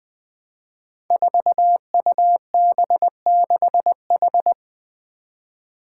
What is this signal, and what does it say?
Morse code, heard as a keyed tone, about 20 wpm, 700 Hz. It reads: 4UB6H
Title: Five Challenging Letter-Number Combinations 20wpm